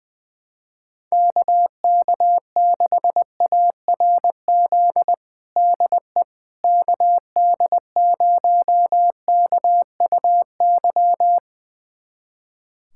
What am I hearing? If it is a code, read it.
Morse code, heard as a keyed tone, about 20 wpm, 700 Hz. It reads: KK6ARZ DE KD0KUY
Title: USA Callsign DE Callsign 20wpm